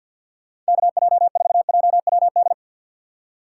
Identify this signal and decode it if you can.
Morse code, heard as a keyed tone, about 66 wpm, spaced as for 58 wpm, 700 Hz. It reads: KJ4JWD
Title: Speed Racing - USA 2x3 Call Signs 50wpm